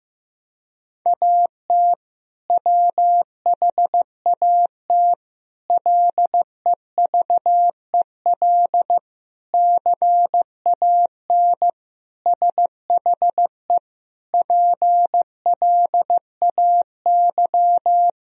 Morse code, heard as a keyed tone, about 16 wpm, 700 Hz. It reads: AT WHAT LEVEL CAN SHE PLAY
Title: Sentences from Top 1000 Words - Review Part 4 - 15wpm